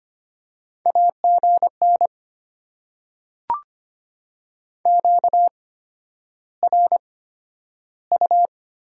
Morse code, heard as a keyed tone, about 25 wpm, spaced as for 11 wpm, 700 Hz. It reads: AGN QRU
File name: Top 90 QSO Elements - 8x Character Spacing - 25wpm